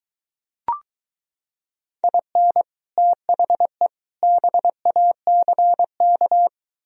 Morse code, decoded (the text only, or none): IN THE BACK